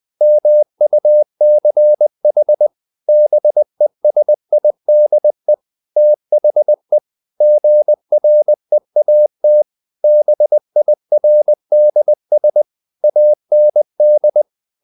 Morse code, heard as a keyed tone, about 20 wpm, 600 Hz. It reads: MUCH BESIDE THE GREAT BIRDS AND
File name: the old man and the sea / the old man and the sea part 1 20wpm